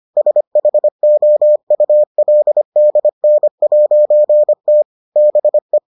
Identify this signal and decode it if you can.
Morse code, heard as a keyed tone, about 25 wpm, 600 Hz. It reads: SHOULDN'T BE